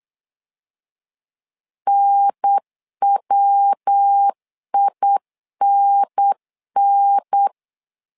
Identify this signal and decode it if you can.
Morse code, heard as a keyed tone, about 8 wpm, 800 Hz. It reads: NWINN
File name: LEZIONI / 3 MWRO / 3 MW 8wpm 2501